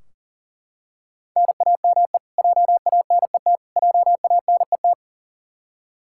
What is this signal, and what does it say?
Morse code, heard as a keyed tone, about 40 wpm, 700 Hz. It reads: NAME JANET JANET
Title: Practice QSO - Exchange 1 - Beginning - 40wpm